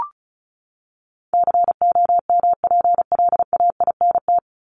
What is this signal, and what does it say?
Morse code, heard as a keyed tone, about 34 wpm, 700 Hz. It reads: COMPLAINT
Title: Top 1251-1500 Words 35wpm